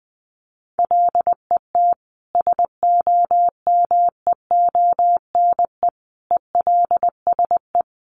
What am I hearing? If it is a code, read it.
Morse code, heard as a keyed tone, about 20 wpm, 700 Hz. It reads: LET SOMEONE ELSE
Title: Sentences from Top 1200 Words 20wpm